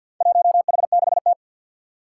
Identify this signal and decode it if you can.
Morse code, heard as a keyed tone, about 48 wpm, 700 Hz. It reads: 1H6T
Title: Four Challenging Letter-Number Combinations 50wpm